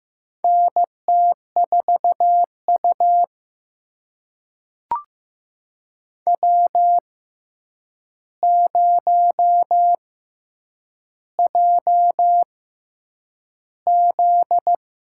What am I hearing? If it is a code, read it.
Morse code, heard as a keyed tone, about 15 wpm, spaced as for 8 wpm, 700 Hz. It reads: NT4U W0JZ
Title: USA Call Signs - 6x Character Spacing - 15wpm